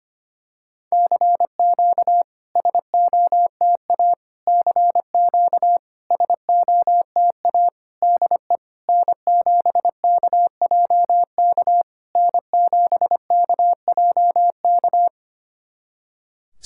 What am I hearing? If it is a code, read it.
Morse code, heard as a keyed tone, about 25 wpm, 690 Hz. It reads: CQ SOTA CQ SOTA DE N7KJK N7KJK
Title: SOTA QSO Practice 25wpm